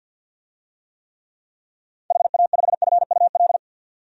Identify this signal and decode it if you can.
Morse code, heard as a keyed tone, about 48 wpm, 700 Hz. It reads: HA5FUL
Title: DXCC-100 Call Signs - 6 Characters - 50wpm